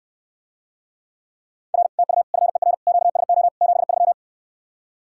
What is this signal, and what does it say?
Morse code, heard as a keyed tone, about 69 wpm, 700 Hz. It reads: R TU FER QSO 73